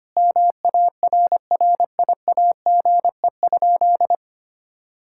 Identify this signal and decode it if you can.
Morse code, heard as a keyed tone, about 25 wpm, 700 Hz. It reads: MARRIAGE?
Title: Sentences from Top 1000 Words - Review Part 5 - 25wpm